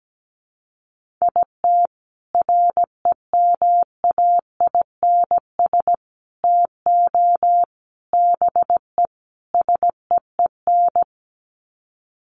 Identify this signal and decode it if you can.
Morse code, heard as a keyed tone, about 17 wpm, 700 Hz. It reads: IT REMAINS TO BE SEEN
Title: Sets of 5 Words Encore 17wpm